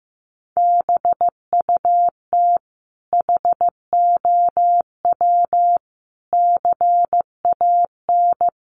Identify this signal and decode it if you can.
Morse code, heard as a keyed tone, about 14 wpm, 700 Hz. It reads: BUT HOW CAN
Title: Sentences from Top 1300 Words 15wpm